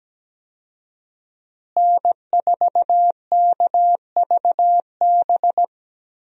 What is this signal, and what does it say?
Morse code, heard as a keyed tone, about 17 wpm, 700 Hz. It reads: N4KVB
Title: USA Call Signs - 5 Characters - Repeated 3x - 17wpm